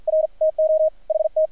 Morse code, heard as a keyed tone, about 50 wpm, 630 Hz, with radio noise. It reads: R TO HT
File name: archived / CRSnet